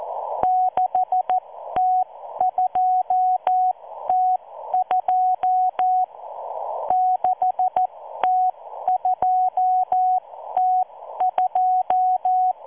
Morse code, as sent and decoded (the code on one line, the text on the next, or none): -.... - ..--- - ..--- / -.... - ..--- - ..---
6T2T2 6T2T2